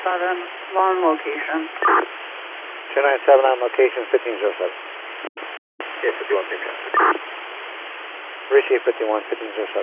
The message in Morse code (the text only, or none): none